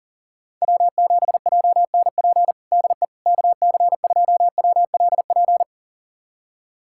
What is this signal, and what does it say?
Morse code, heard as a keyed tone, about 37 wpm, 700 Hz. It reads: W7JNP DE KC2WLP
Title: USA Callsign DE Callsign 40wpm